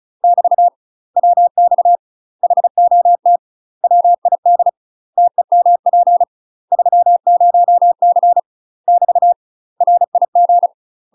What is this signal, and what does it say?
Morse code, heard as a keyed tone, about 34 wpm, 700 Hz. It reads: <BT> WX HOT WID TEMP 30C <BT> RIG
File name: UK MASTER QSOs TRAINING / 35WPM / UK MASTER QSOs 0040